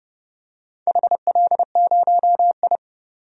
Morse code, heard as a keyed tone, about 29 wpm, 700 Hz. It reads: HL0I
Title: DXCC-100 Call Signs - 4 Characters - 30wpm